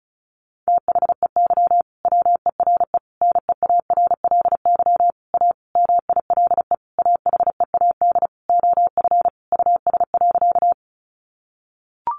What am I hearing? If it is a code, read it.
Morse code, heard as a keyed tone, about 34 wpm, 700 Hz. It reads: THEY WERE NEARLY A MILE AHEAD OF US.
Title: Sentences from Top 1300 Words 35wpm